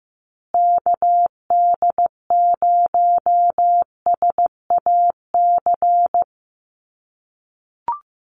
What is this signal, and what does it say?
Morse code, heard as a keyed tone, about 15 wpm, 700 Hz. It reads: KD0SAC
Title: USA Call Signs Repeated 2x 15wpm